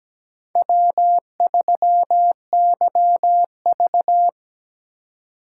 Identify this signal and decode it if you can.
Morse code, heard as a keyed tone, about 20 wpm, 700 Hz. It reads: W3YV
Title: Speed Racing - USA Call Signs 17wpm